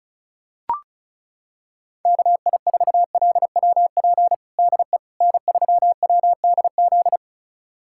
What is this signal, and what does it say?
Morse code, heard as a keyed tone, about 34 wpm, 700 Hz. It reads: KI4LWP DE N3WDZ
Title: USA Callsign DE Callsign 35wpm